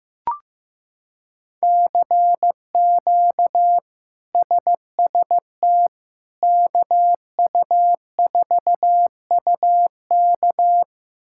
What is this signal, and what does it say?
Morse code, heard as a keyed tone, about 15 wpm, 700 Hz. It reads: CQ SST KU4UK
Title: K1USN SST QSO Practice - 15wpm